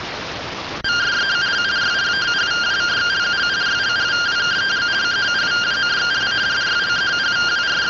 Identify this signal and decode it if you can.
Two-tone teleprinter signal, about 170 Hz shift.